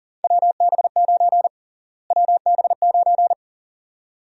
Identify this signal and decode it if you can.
Morse code, heard as a keyed tone, about 40 wpm, 700 Hz. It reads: WB9 WB9
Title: USA Call Sign Prefixes - Repeated 2x - 40wpm